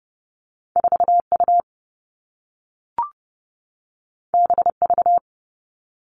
Morse code, heard as a keyed tone, about 30 wpm, 700 Hz. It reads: 4 U B V
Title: Two Challenging Letter-Number Combinations 30wpm